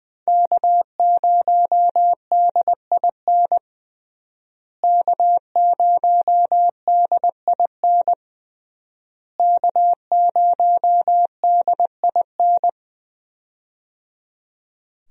Morse code, heard as a keyed tone, about 21 wpm, 700 Hz. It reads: K0DIN K0DIN K0DIN
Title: USA Call Signs - 5 Characters - Repeated 3x - 20wpm